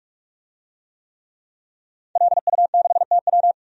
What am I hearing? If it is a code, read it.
Morse code, heard as a keyed tone, about 46 wpm, 700 Hz. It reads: LU6TW
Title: DXCC-100 Call Signs - 5 Characters - 45wpm